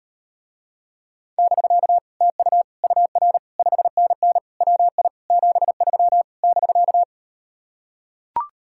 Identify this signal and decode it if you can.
Morse code, heard as a keyed tone, about 38 wpm, 700 Hz. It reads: <BK> TU UR 5NN WI 73 <BK>